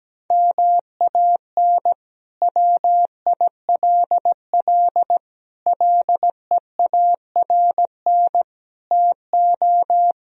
Morse code, heard as a keyed tone, about 17 wpm, 700 Hz. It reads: MAN WILL LEARN TO
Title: Sentences from Top 1000 Words - Review Part 2 - 17wpm